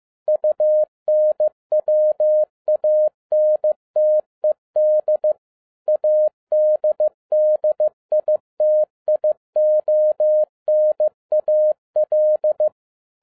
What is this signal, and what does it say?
Morse code, heard as a keyed tone, about 15 wpm, 610 Hz. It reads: UNWANTED ADDITIONAL